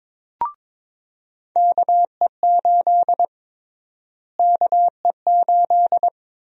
Morse code, heard as a keyed tone, about 22 wpm, 700 Hz. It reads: KE8 KE8
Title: USA Call Sign Prefixes - Repeated 2x - 22wpm